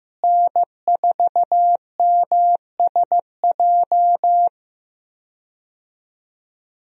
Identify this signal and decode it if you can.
Morse code, heard as a keyed tone, about 15 wpm, 700 Hz. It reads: N4MSJ
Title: USA Call Signs - 5 Characters - Repeated 2x - 15wpm